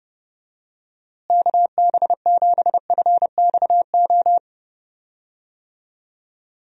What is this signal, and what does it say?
Morse code, heard as a keyed tone, about 30 wpm, 700 Hz. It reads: KB7FXO
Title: USA Call Signs - 6 Characters - Repeated 2x - 30wpm